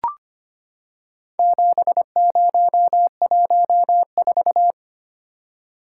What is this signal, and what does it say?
Morse code, heard as a keyed tone, about 25 wpm, 700 Hz. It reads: 7014
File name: practice / 25wpm / Four Letter-Number Combinations - 25wpm